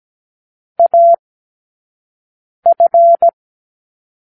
Morse code, heard as a keyed tone, about 17 wpm, spaced as for 5 wpm, 680 Hz. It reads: AF
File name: k7qo / Lesson12(A-H)